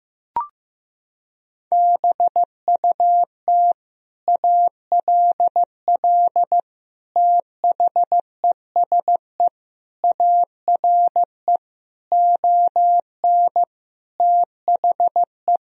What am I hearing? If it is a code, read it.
Morse code, heard as a keyed tone, about 15 wpm, 700 Hz. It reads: BUT ALL THESE ARE ON THE
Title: Sentences from Top 1000 Words - Review Part 3 - 15wpm